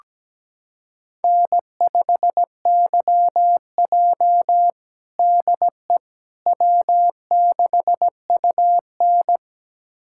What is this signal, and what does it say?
Morse code, heard as a keyed tone, about 17 wpm, 700 Hz. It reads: N5YJ DE W6UN
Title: USA Callsign DE Callsign - 4 Characters - 17wpm